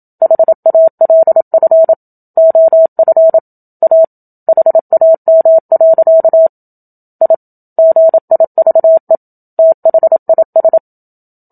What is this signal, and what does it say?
Morse code, heard as a keyed tone, about 27 wpm, 650 Hz. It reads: HALF OF A HAM. I GIVE THIS
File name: Book Two / 01-Chapter 1 - 27 WPM